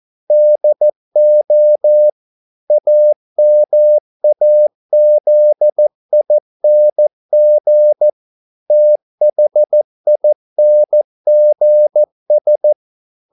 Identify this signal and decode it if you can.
Morse code, heard as a keyed tone, about 14 wpm, 600 Hz. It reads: DO AMAZING THINGS